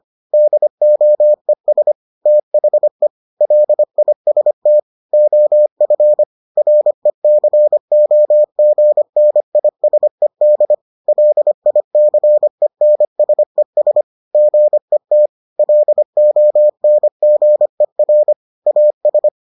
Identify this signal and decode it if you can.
Morse code, heard as a keyed tone, about 25 wpm, 600 Hz. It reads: DOES THE LIST OF RECOGNISED LICENSES GET LONGER AS